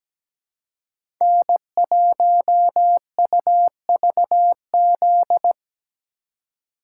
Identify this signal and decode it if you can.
Morse code, heard as a keyed tone, about 17 wpm, 700 Hz. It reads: N1UVZ
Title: USA Call Signs - 4x Character Spacing - 17wpm